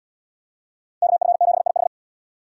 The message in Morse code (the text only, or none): KF7IH